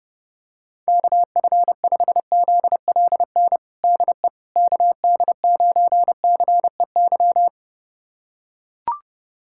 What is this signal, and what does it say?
Morse code, heard as a keyed tone, about 30 wpm, 690 Hz. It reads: KF5ZLN DE KD9CEY
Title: USA Callsign DE Callsign 30wpm